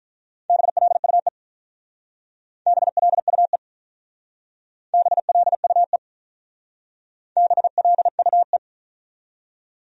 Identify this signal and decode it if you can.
Morse code, heard as a keyed tone, about 44 wpm, 700 Hz. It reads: BLUE BLUE BLUE BLUE